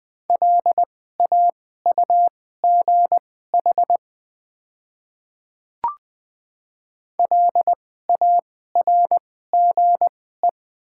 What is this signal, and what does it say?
Morse code, heard as a keyed tone, about 20 wpm, spaced as for 14 wpm, 700 Hz. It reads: LAUGH LARGE
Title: 2x Character Spacing - Top 500 Words - 20wpm